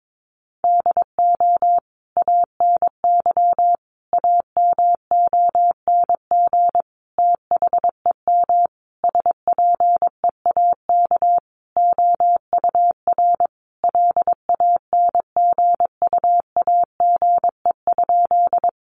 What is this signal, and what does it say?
Morse code, heard as a keyed tone, about 22 wpm, 700 Hz. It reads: DO ANY AMONG THEM SPEAK OUR LANGUAGE?
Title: Sentences from Top 1100 Words 22wpm